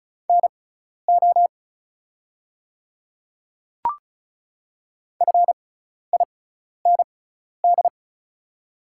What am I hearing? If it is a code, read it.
Morse code, heard as a keyed tone, about 36 wpm, spaced as for 12 wpm, 700 Hz. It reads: NO FIND
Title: 6x Character Spacing - Top 100 Words - 35wpm